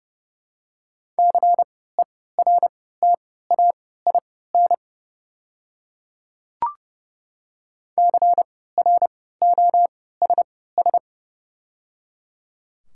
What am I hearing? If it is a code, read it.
Morse code, heard as a keyed tone, about 31 wpm, spaced as for 17 wpm, 690 Hz. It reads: CERTAIN CROSS
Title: 3x Character Spacing - Top 500 Words - 30wpm